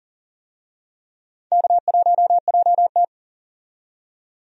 Morse code, heard as a keyed tone, about 40 wpm, 700 Hz. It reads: K1JT
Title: USA Call Signs - 4 Characters - 40wpm